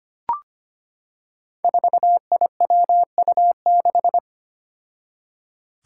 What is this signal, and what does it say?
Morse code, heard as a keyed tone, about 25 wpm, 700 Hz. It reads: 4IWU6
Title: Five Challenging Letter-Number Combinations 25wpm